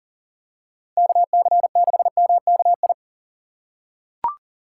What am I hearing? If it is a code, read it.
Morse code, heard as a keyed tone, about 40 wpm, 700 Hz. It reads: KC6MKI
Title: USA Call Signs - 6x Character Spacing - 40wpm